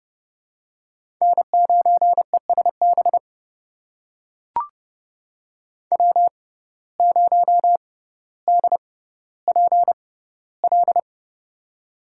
Morse code, heard as a keyed tone, about 32 wpm, spaced as for 16 wpm, 700 Hz. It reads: N9ESB W0DPL
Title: USA Call Signs - 5 Characters - 6x Character Spacing - 30wpm